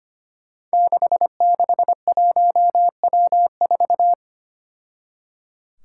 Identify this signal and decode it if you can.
Morse code, heard as a keyed tone, about 25 wpm, 700 Hz. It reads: 661W4
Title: Five Challenging Letter-Number Combinations 25wpm